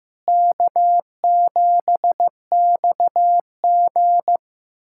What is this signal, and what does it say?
Morse code, heard as a keyed tone, about 16 wpm, 700 Hz. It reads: K7XG
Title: USA Callsign DE Callsign - 4 Characters - 15wpm